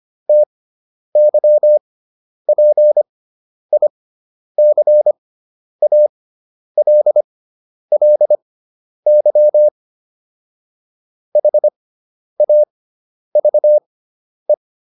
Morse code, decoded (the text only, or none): TYPICALLY HAVE